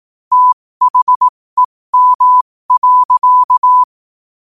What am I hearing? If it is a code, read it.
Morse code, heard as a keyed tone, about 18 wpm, spaced as for 15 wpm, 1000 Hz. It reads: THEM.
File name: qotdfiles / 15wpm